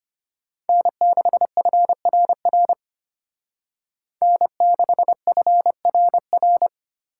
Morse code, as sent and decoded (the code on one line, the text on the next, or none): -. -.... ..-. .-. .-. / -. -.... ..-. .-. .-.
N6FRR N6FRR